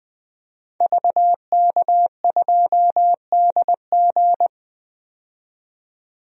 Morse code, heard as a keyed tone, about 20 wpm, 700 Hz. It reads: VK2DG
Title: DXCC-100 Call Signs - 5 Characters - 20wpm